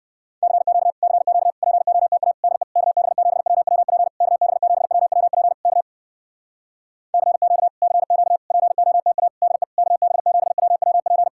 Morse code, heard as a keyed tone, about 59 wpm, 700 Hz. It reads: CQ CQ POTA DE KB7WWP KB7WWP K CQ CQ POTA DE KB7WWP